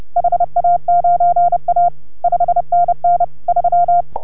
Morse code, heard as a keyed tone, about 31 wpm, 690 Hz. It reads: HA9A 5NN 3